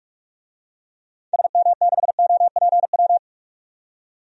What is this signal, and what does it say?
Morse code, heard as a keyed tone, about 41 wpm, 700 Hz. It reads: SM6OPW
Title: DXCC-100 Call Signs - 6 Characters - 45wpm